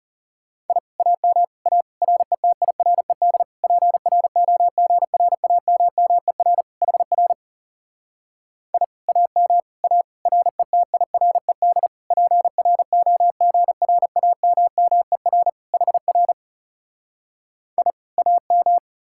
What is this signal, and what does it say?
Morse code, heard as a keyed tone, about 38 wpm, 700 Hz. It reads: I AM A RETIRED PROGRAMMER HR I AM A RETIRED PROGRAMMER HR I AM